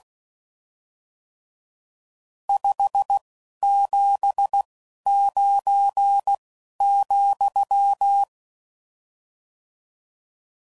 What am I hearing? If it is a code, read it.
Morse code, heard as a keyed tone, about 16 wpm, spaced as for 11 wpm, 800 Hz. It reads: 579,
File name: CW Code Groups @10WPM / RSTs / Various RST No's